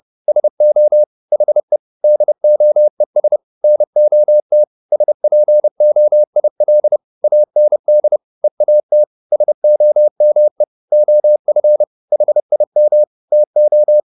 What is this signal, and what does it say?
Morse code, heard as a keyed tone, about 30 wpm, 600 Hz. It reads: SO HE DOES NOT SPOIL AND EAT SOME OF HIM TO